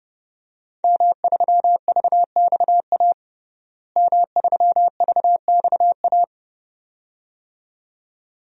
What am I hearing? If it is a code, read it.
Morse code, heard as a keyed tone, about 29 wpm, 700 Hz. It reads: M3VXA M3VXA
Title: DXCC-100 Call Signs - 5 Characters - Repeated 2x - 30wpm